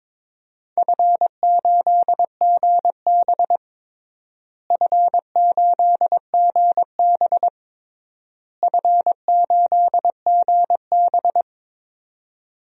Morse code, heard as a keyed tone, about 22 wpm, 700 Hz. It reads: F8GB F8GB F8GB
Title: DXCC-100 Call Signs - 4 Characters - Repeated 3x - 22wpm